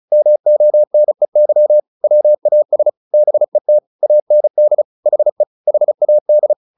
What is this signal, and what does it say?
Morse code, heard as a keyed tone, about 34 wpm, 600 Hz. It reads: MONEY WAS BET AND HE HAD
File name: the old man and the sea 2 35wpm